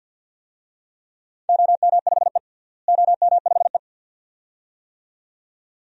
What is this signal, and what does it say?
Morse code, heard as a keyed tone, about 51 wpm, 700 Hz. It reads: OM5E OM5E
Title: DXCC-100 Call Signs - 4 Characters - Repeated 2x - 50wpm